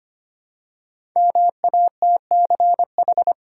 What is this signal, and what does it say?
Morse code, heard as a keyed tone, about 25 wpm, 700 Hz. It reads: MATCH
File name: 5 Letter Words - First 3 Letters Given - 25wpm